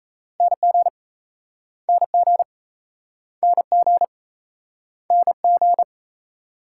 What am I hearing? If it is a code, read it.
Morse code, heard as a keyed tone, about 37 wpm, 700 Hz. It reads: NG NG NG NG